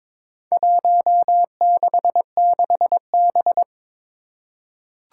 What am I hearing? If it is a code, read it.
Morse code, heard as a keyed tone, about 22 wpm, 700 Hz. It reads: 166B
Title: Four Challenging Letter-Number Combinations 22wpm